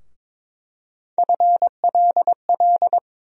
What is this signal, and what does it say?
Morse code, heard as a keyed tone, about 22 wpm, 700 Hz. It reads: FLL
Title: USA Call Sign Postfixes - 3 Letters 22wpm